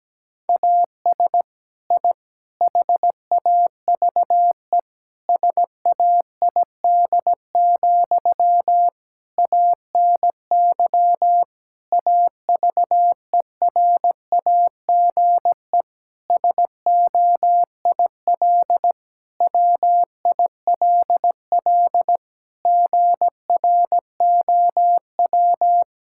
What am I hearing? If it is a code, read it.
Morse code, heard as a keyed tone, about 17 wpm, 700 Hz. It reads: AS I HAVE SAID, ANY AVERAGE SOIL WILL GROW
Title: Sentences from Top 1000 Words - Review Part 6 - 17wpm